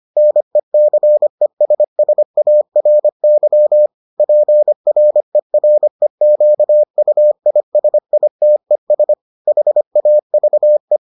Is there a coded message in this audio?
Yes: Morse code from 0.2 to 11.0 s, heard as a keyed tone, about 25 wpm, 600 Hz. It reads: NECESSARY PREREQUISITES HAVE